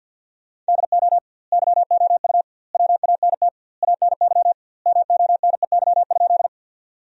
Morse code, heard as a keyed tone, about 50 wpm, 700 Hz. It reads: DO YOU WANT ANY MONEY?
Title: 2x Word-Spacing - Sentences from Top 400 Words - 50wpm